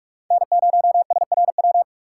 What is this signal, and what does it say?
Morse code, heard as a keyed tone, about 43 wpm, 700 Hz. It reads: N0SRW
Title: USA Call Signs - 3x Character Spacing - 45wpm